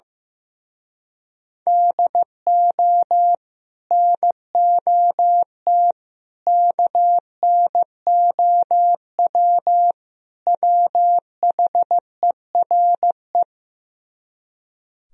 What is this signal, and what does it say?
Morse code, heard as a keyed tone, about 16 wpm, 700 Hz. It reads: DO NOT KNOW WHERE
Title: Speed-Racing - Sets of 4 Words 15wpm